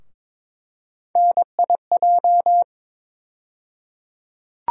Morse code, heard as a keyed tone, about 22 wpm, 700 Hz. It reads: NIJ
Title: USA Call Sign Postfixes - 3 Letters 22wpm